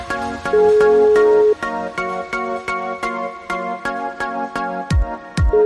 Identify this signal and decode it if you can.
Ringback tone (Europe, 425 Hz), with noise.